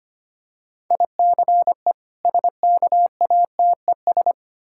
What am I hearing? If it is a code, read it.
Morse code, heard as a keyed tone, about 25 wpm, 690 Hz. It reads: ICE SKATES
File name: practice / 25wpm / Difficult Words with Hints - 25wpm